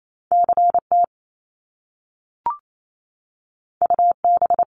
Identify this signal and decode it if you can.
Morse code, heard as a keyed tone, about 28 wpm, 700 Hz. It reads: CT UB
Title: Top Two Letter Combinations in Words 28wpm